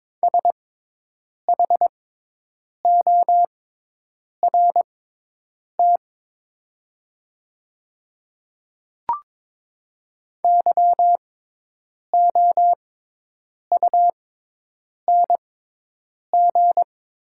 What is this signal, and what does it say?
Morse code, heard as a keyed tone, about 22 wpm, spaced as for 8 wpm, 700 Hz. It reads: SHORT YOUNG
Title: 6x Character Spacing - Top 500 Words - 22wpm